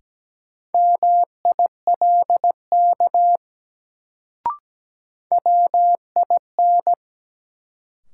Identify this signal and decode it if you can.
Morse code, heard as a keyed tone, about 17 wpm, 700 Hz. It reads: MILK WIN
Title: Sentences from Top 1500 Words - New Word Drill - 17wpm